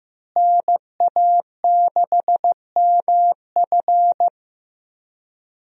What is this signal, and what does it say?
Morse code, heard as a keyed tone, about 15 wpm, 700 Hz. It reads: NA6MF
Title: USA Call Signs - 5 Characters - Repeated 2x - 15wpm